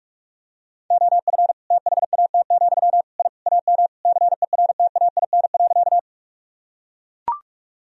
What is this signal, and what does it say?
Morse code, heard as a keyed tone, about 41 wpm, 700 Hz. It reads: OF THAT, I AM CERTAIN.
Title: Sentences from Top 1000 Words - Review Part 2 - 45wpm